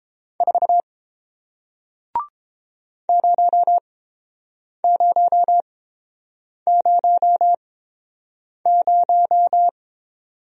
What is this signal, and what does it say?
Morse code, heard as a keyed tone, about 30 wpm, spaced as for 7 wpm, 700 Hz. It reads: 4 0000